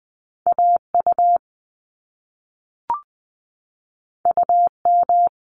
Morse code, heard as a keyed tone, about 20 wpm, 700 Hz. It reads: AU UM
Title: Top Two Letter Combinations in Words 20wpm